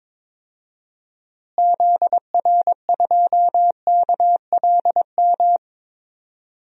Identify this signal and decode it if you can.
Morse code, heard as a keyed tone, about 21 wpm, 700 Hz. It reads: ZR2KLM
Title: DXCC-100 Call Signs - 6 Characters - Repeated 2x - 22wpm